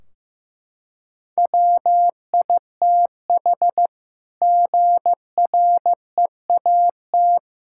Morse code, heard as a keyed tone, about 15 wpm, 700 Hz. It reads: WITH GREAT
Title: Sentences from Top 1600 Words 15wpm